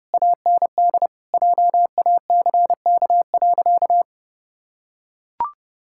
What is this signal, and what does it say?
Morse code, heard as a keyed tone, about 30 wpm, 700 Hz. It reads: AND JACK.
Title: Pangram Challenge v2 - 30wpm